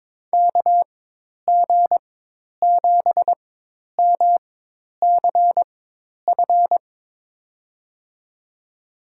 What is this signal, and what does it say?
Morse code, heard as a keyed tone, about 22 wpm, spaced as for 10 wpm, 700 Hz. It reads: KG7MCF
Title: USA Call Signs - 6 Characters - 4x Character Spacing - 22wpm